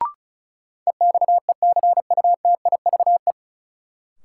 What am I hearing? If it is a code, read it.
Morse code, heard as a keyed tone, about 36 wpm, 700 Hz. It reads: EXECUTIVE
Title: Sets of 5 Words Encore - New Word Drill - 35wpm